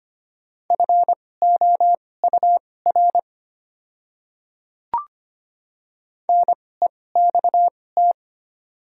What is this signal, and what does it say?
Morse code, heard as a keyed tone, about 25 wpm, spaced as for 18 wpm, 690 Hz. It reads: FOUR NEXT
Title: 2x Character Spacing - Top 500 Words - 25wpm